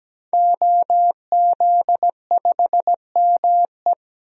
Morse code, heard as a keyed tone, about 17 wpm, 700 Hz. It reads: OZ5ME
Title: DXCC-100 Call Signs - 5 Characters - 17wpm